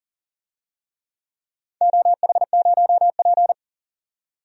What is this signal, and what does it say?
Morse code, heard as a keyed tone, about 40 wpm, 700 Hz. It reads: OH0P
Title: DXCC-100 Call Signs - 4 Characters - 40wpm